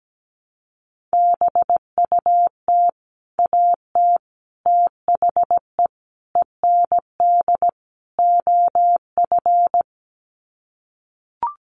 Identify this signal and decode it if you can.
Morse code, heard as a keyed tone, about 17 wpm, 700 Hz. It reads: BUT AT THE END OF
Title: Sets of 5 Words Encore2 17wpm